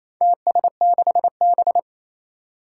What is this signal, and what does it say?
Morse code, heard as a keyed tone, about 27 wpm, 700 Hz. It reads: TS6B
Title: Four Challenging Letter-Number Combinations 28wpm